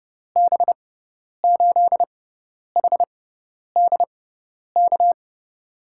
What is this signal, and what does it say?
Morse code, heard as a keyed tone, about 30 wpm, spaced as for 10 wpm, 700 Hz. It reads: B8HDK
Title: USA Call Signs - 6x Character Spacing - 30wpm